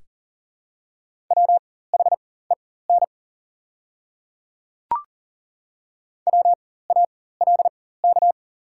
Morse code, heard as a keyed tone, about 39 wpm, spaced as for 19 wpm, 700 Hz. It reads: WHEN WALK
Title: 4x Character Spacing - Top 300 Words - 40wpm